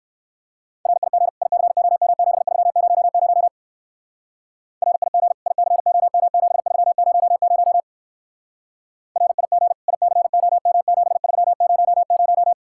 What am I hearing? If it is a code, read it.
Morse code, heard as a keyed tone, about 56 wpm, 700 Hz. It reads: RIG ICOM7300 RIG ICOM7300 RIG ICOM7300